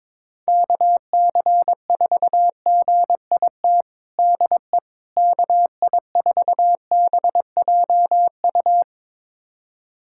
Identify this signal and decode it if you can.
Morse code, heard as a keyed tone, about 22 wpm, 700 Hz. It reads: KC4GIT DE KI4BJU
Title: USA Callsign DE Callsign 22wpm